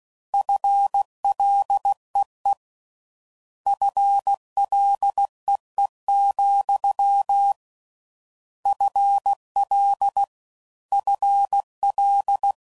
Morse code, heard as a keyed tone, about 16 wpm, 800 Hz. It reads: FLEE FLEE, FL FL